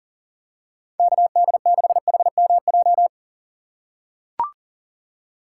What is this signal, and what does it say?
Morse code, heard as a keyed tone, about 40 wpm, 700 Hz. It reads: KD6HMJ
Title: USA Call Signs - 6 Characters - 6x Character Spacing - 40wpm